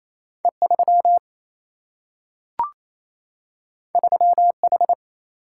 Morse code, heard as a keyed tone, about 28 wpm, 700 Hz. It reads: E3 3H
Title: Single-Letter Single-Number 28wpm